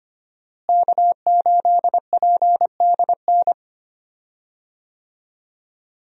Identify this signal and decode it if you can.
Morse code, heard as a keyed tone, about 25 wpm, 700 Hz. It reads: K8PDN
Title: USA Call Signs - 5 Characters - Repeated 2x - 25wpm